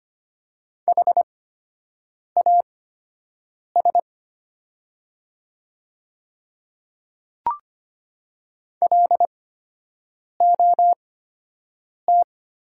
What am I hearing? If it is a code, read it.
Morse code, heard as a keyed tone, about 25 wpm, spaced as for 7 wpm, 700 Hz. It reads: HAS LOT
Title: 8x Character Spacing - Top 500 Words - 25wpm